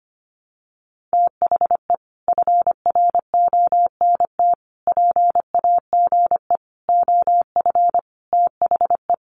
Morse code, heard as a keyed tone, about 25 wpm, 700 Hz. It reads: THE FRONT PAGE OF THE
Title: Sets of 5 Words Encore 25wpm